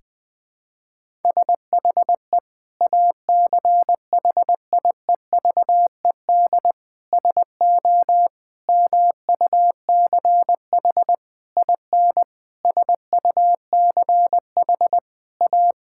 Morse code, heard as a keyed tone, about 21 wpm, 700 Hz. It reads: SHE ACHIEVED SO MUCH IN SUCH A
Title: Sentences with 7-Word Phrases - Spoken Beginning - 20wpm